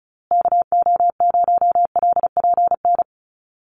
Morse code, heard as a keyed tone, about 34 wpm, 700 Hz. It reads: KO0LPN
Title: USA Call Signs Repeated 3x 35wpm